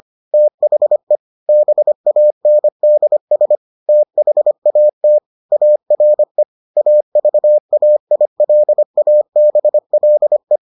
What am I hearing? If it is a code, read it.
Morse code, heard as a keyed tone, about 25 wpm, 600 Hz. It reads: THE BANDS THAT ARE AVAILABLE